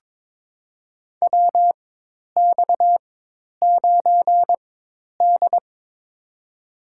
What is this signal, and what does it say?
Morse code, heard as a keyed tone, about 22 wpm, spaced as for 10 wpm, 700 Hz. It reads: WX9D
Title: USA Call Signs - 4x Character Spacing - 22wpm